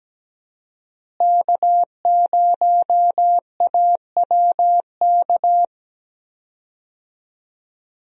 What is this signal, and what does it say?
Morse code, heard as a keyed tone, about 17 wpm, 700 Hz. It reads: K0AWK
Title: USA Call Signs - 5 Characters - Repeated 3x - 17wpm